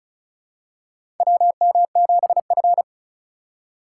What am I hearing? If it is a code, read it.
Morse code, heard as a keyed tone, about 34 wpm, 700 Hz. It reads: WM7F
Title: USA Call Signs - 4 Characters - 3x Character Spacing - 35wpm